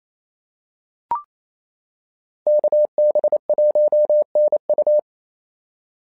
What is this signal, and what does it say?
Morse code, heard as a keyed tone, about 27 wpm, 590 Hz. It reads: KB1NU
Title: SOTA QSO Practice 28wpm